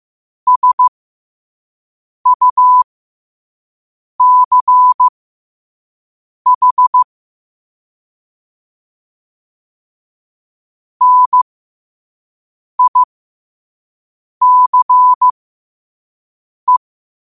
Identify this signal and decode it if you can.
Morse code, heard as a keyed tone, about 15 wpm, spaced as for 5 wpm, 1000 Hz. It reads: SUCH NICE
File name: qotdfiles / 5wpm